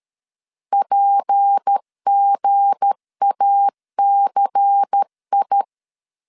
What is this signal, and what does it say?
Morse code, heard as a keyed tone, about 13 wpm, 800 Hz. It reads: PGACI
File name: CW / DET / 1 9 1500C S A 12wpm 3800